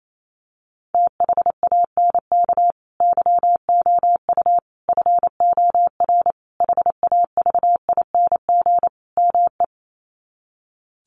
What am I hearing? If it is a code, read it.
Morse code, heard as a keyed tone, about 28 wpm, 700 Hz. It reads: THANK YOU FOR HAVING ME
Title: Sets of 5 Words Encore 28wpm